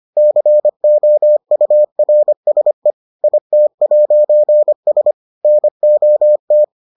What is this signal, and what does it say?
Morse code, heard as a keyed tone, about 25 wpm, 600 Hz. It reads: COURSE IT'S NOT